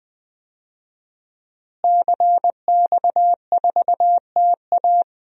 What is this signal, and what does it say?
Morse code, heard as a keyed tone, about 20 wpm, 700 Hz. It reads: CX4TA
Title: DXCC-100 Call Signs - 5 Characters - Repeated 3x - 20wpm